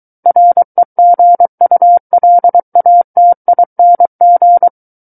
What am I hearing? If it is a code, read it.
Morse code, heard as a keyed tone, about 23 wpm, 700 Hz. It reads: REGULATING
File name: Book One / 14-Chapter 14 - 23 WPM